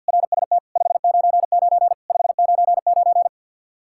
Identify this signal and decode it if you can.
Morse code, heard as a keyed tone, about 47 wpm, 700 Hz. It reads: RST 599 599
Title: Practice QSO - Exchange 1 - Beginning - 50wpm